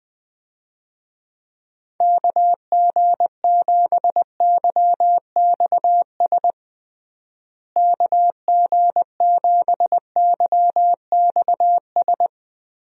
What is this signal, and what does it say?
Morse code, heard as a keyed tone, about 20 wpm, 700 Hz. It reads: KG7YXS KG7YXS
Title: USA Call Signs - 6 Characters - Repeated 3x - 20wpm